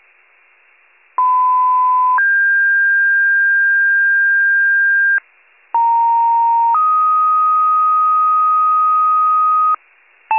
Two-tone page, heard as a one-second tone, then a three-second tone, about 1010 then 1600 Hz, about 930 then 1250 Hz, with noise. It is heard from 1.2 to 9.8 s.
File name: pager / 94 RESCUE DUTY / wcalert